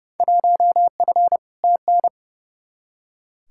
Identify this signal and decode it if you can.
Morse code, heard as a keyed tone, about 31 wpm, 700 Hz. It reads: 1F TN
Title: Field Day Exchanges - 30wpm